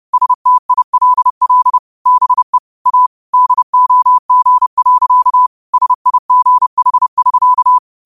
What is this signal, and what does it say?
Morse code, heard as a keyed tone, about 30 wpm, 1000 Hz. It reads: STILL BE A DOG. SIGH<SK>
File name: qotdfiles / 30wpm